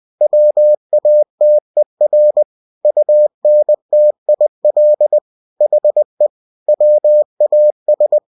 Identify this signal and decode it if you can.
Morse code, heard as a keyed tone, about 20 wpm, 600 Hz. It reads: WATER UNTIL HE WAS
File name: cw-books / winnie-the-pooh / winnie-20wpm-page-0101